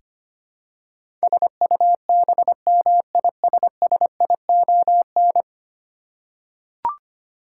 Morse code, heard as a keyed tone, about 25 wpm, 700 Hz. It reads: SUBMISSION
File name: practice / 25wpm / Ten Letter Words - 25wpm